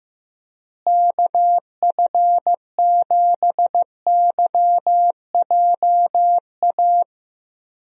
Morse code, heard as a keyed tone, about 15 wpm, 700 Hz. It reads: KF7YJA
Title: USA Call Signs - 6 Characters - Repeated 3x - 15wpm